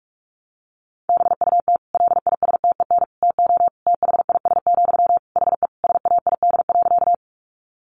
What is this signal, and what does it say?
Morse code, heard as a keyed tone, about 44 wpm, 700 Hz. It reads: BUT LISTEN TO THIS, HE SAID.
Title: Sentences from Top 400 Words 45wpm